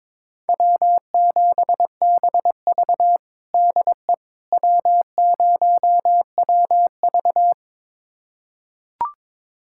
Morse code, heard as a keyed tone, about 22 wpm, 700 Hz. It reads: W7BV DE W0WV
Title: USA Callsign DE Callsign - 4 Characters - 22wpm